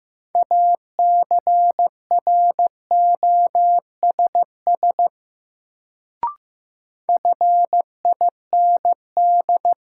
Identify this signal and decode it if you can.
Morse code, heard as a keyed tone, about 15 wpm, 700 Hz. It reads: ACROSS FIND
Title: Sets of 2 Words - New Word Drill - 15wpm